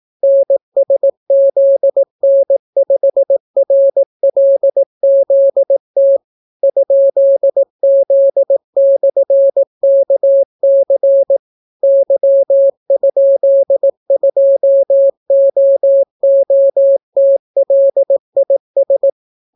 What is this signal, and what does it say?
Morse code, heard as a keyed tone, about 18 wpm, 550 Hz. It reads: NSZN5RLZT ?Z/KC Y?2OOTLIS